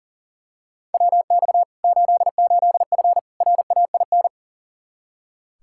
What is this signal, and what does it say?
Morse code, heard as a keyed tone, about 40 wpm, 700 Hz. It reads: WX 88F RAIN
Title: Practice QSO - Exchange 2 - Info - 40wpm